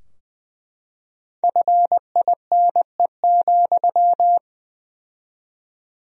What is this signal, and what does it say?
Morse code, heard as a keyed tone, about 20 wpm, 700 Hz. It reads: FINE,
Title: 4x Word-Spacing - Sentences from Top 500 Words - 20wpm